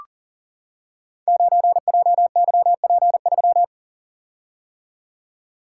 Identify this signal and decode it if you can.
Morse code, heard as a keyed tone, about 39 wpm, 700 Hz. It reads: 9JYP3
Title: Code Groups - 5 Letter:Numbers - 40wpm